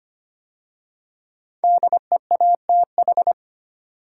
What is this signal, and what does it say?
Morse code, heard as a keyed tone, about 25 wpm, 700 Hz. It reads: DEATH
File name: 5 Letter Words - First 3 Letters Given - 25wpm